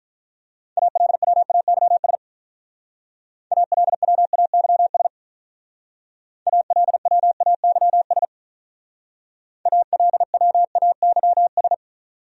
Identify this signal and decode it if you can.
Morse code, heard as a keyed tone, about 43 wpm, 700 Hz. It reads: ALWAYS ALWAYS ALWAYS ALWAYS
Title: Speed-Racing Top 300 Words 35wpm